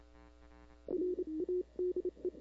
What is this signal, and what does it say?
Morse code, heard as a keyed tone, about 29 wpm, spaced as for 26 wpm, 350 Hz. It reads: RADE
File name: CW / 2005-k5tr arrl cw dx -00:00